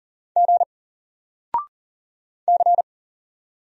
Morse code, heard as a keyed tone, about 40 wpm, 700 Hz. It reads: G C
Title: Single Letter-Number 40wpm